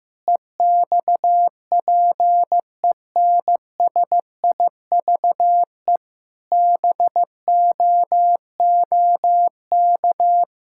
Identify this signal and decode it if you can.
Morse code, heard as a keyed tone, about 15 wpm, 700 Hz. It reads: EXPENSIVE BOOK
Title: Sentences from Top 1000 Words - Review Part 6 - 15wpm